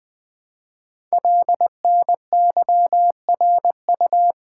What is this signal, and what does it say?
Morse code, heard as a keyed tone, about 20 wpm, 700 Hz. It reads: LNYRU
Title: Code Groups - 5 Letter:Numbers - 20wpm